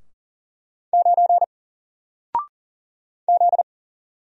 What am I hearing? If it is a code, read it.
Morse code, heard as a keyed tone, about 41 wpm, 700 Hz. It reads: 9 Z
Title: Single Letter-Number Alt 40wpm